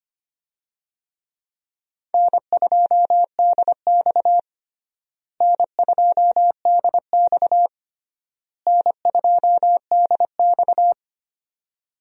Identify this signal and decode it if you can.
Morse code, heard as a keyed tone, about 25 wpm, 700 Hz. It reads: N2DX N2DX N2DX
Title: USA Call Signs - 4 Characters - Repeated 3x - 25wpm